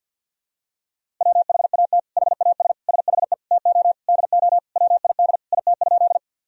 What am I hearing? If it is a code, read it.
Morse code, heard as a keyed tone, about 52 wpm, 700 Hz. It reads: WHAT HAS SHE TO DO WID IT?
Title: Sentences from Top 100 Words and Abbreviations 50wpm